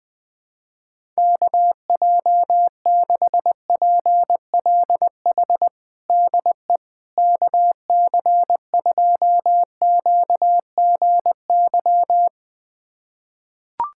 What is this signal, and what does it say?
Morse code, heard as a keyed tone, about 20 wpm, 700 Hz. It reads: KJ6PLH DE KC2QGY
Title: USA Callsign DE Callsign - 6 Characters - 20wpm